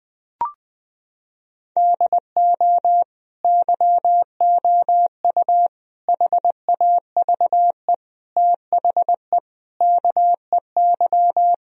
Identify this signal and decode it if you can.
Morse code, heard as a keyed tone, about 20 wpm, 700 Hz. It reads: DO YOU HAVE THE KEY
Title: Sentences from Top 1000 Words - Review Part 4 - 20wpm